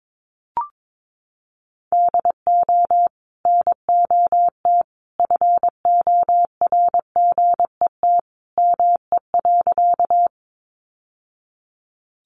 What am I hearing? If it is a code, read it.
Morse code, heard as a keyed tone, about 22 wpm, 700 Hz. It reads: DO NOT FORGET ME.
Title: Sentences from Top 1200 Words 22wpm